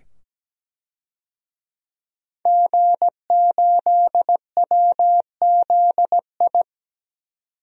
Morse code, heard as a keyed tone, about 17 wpm, 700 Hz. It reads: G8WZI
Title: DXCC-100 Call Signs - 5 Characters - 17wpm